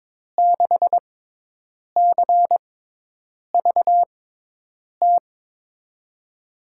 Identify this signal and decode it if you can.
Morse code, heard as a keyed tone, about 22 wpm, spaced as for 8 wpm, 700 Hz. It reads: 6CVT
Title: USA Call Signs - 6 Characters - 6x Character Spacing - 22wpm